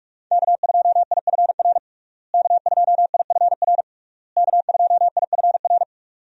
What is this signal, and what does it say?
Morse code, heard as a keyed tone, about 41 wpm, 700 Hz. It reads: K2IFR K2IFR K2IFR
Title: USA Call Signs - 5 Characters - Repeated 3x - 45wpm